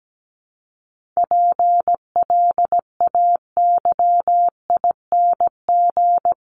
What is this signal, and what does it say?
Morse code, heard as a keyed tone, about 17 wpm, 700 Hz. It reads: PLAYING
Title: ING Suffix - Top 100 Words with Base Words 17wpm